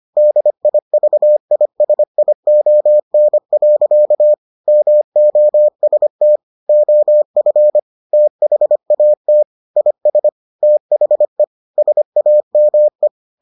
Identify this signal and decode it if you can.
Morse code, heard as a keyed tone, about 25 wpm, 600 Hz. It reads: DIVISION. MOST OF THAT IS THE SAME